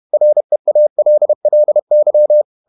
Morse code, heard as a keyed tone, about 31 wpm, 600 Hz. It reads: REALLY
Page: Never 31wpm | CW med Gnister